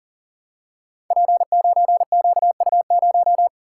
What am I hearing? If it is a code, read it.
Morse code, heard as a keyed tone, about 40 wpm, 690 Hz. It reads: P9QU0
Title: Code Groups - 5 Letter:Numbers - 40wpm